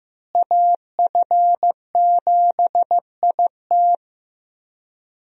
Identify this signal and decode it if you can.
Morse code, heard as a keyed tone, about 15 wpm, 700 Hz. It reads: AF7IT
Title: USA Call Signs - 5 Characters - Repeated 3x - 15wpm